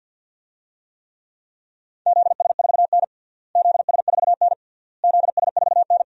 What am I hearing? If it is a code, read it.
Morse code, heard as a keyed tone, about 53 wpm, 700 Hz. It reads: ZS4N ZS4N ZS4N
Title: DXCC-100 Call Signs - 4 Characters - Repeated 3x - 50wpm